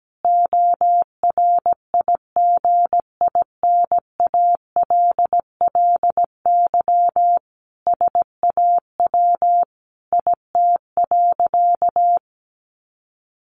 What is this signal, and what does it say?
Morse code, heard as a keyed tone, about 17 wpm, 700 Hz. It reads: ORIGINALLY SAW IT.
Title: Sentences from Top 1500 Words 17wpm